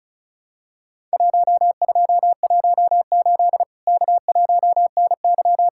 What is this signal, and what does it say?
Morse code, heard as a keyed tone, about 36 wpm, 700 Hz. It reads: 1218 K1NY
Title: Worked All Europe DX Contest - QTC Exchange - 35wpm